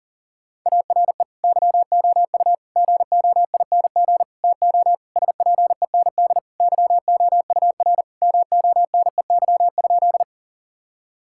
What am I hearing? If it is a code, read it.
Morse code, heard as a keyed tone, about 40 wpm, 700 Hz. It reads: ARE YOU GOING TO SPEND YOUR MONEY?
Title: Sentences from Top 1000 Words - Review Part 5 - 40wpm